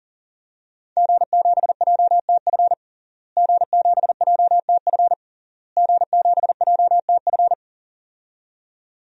Morse code, heard as a keyed tone, about 40 wpm, 700 Hz. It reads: G7JTF G7JTF G7JTF
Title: DXCC-100 Call Signs - 5 Characters - Repeated 3x - 40wpm